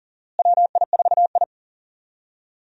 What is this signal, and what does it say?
Morse code, heard as a keyed tone, about 40 wpm, 700 Hz. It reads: WI4I
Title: USA Call Signs - 4 Characters - Repeated 3x - 40wpm